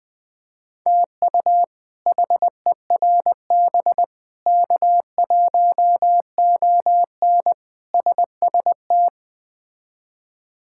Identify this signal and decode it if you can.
Morse code, heard as a keyed tone, about 20 wpm, 700 Hz. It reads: TU HERB K1ON SST
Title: K1USN SST QSO Practice - 20wpm